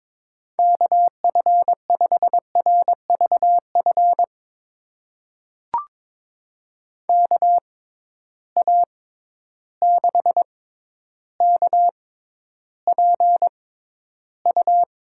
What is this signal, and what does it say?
Morse code, heard as a keyed tone, about 22 wpm, spaced as for 11 wpm, 700 Hz. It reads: KF5RVF KA6KPU
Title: USA Call Signs - 6 Characters - 6x Character Spacing - 22wpm